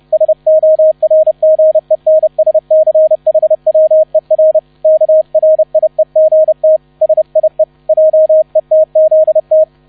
Morse code, heard as a keyed tone, about 30 wpm, 630 Hz, with radio noise. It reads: SORGENSCHWER KRIEGT SIE JETZT